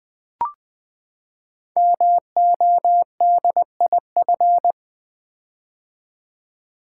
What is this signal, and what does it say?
Morse code, heard as a keyed tone, about 20 wpm, 700 Hz. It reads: MODIF
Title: Build A Word - 5 Letters At A Time - 12-Letter Words - 20wpm